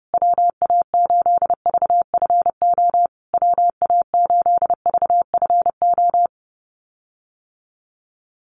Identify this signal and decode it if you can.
Morse code, heard as a keyed tone, about 30 wpm, 700 Hz. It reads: WA8VFO WA8VFO
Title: USA Call Signs Repeated 3x 30wpm